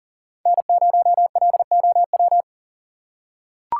Morse code, heard as a keyed tone, about 40 wpm, 700 Hz. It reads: N0LOW
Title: USA Call Signs - 5 Characters - 3x Character Spacing - 40wpm